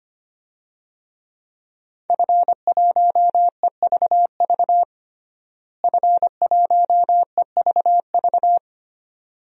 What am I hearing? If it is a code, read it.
Morse code, heard as a keyed tone, about 25 wpm, 700 Hz. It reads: F1EVV F1EVV
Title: DXCC-100 Call Signs - 5 Characters - Repeated 2x - 25wpm